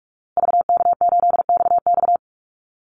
Morse code, heard as a keyed tone, about 44 wpm, 700 Hz. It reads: VK8XX
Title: Call Signs Repeated 2x - 45wpm